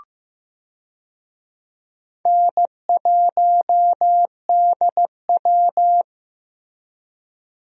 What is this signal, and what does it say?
Morse code, heard as a keyed tone, about 16 wpm, 700 Hz. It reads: N1DW
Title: USA Call Signs - 4 Characters - Repeated 2x - 15wpm